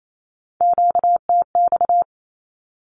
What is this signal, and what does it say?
Morse code, heard as a keyed tone, about 28 wpm, 700 Hz. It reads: QTX
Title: Q-Codes 28wpm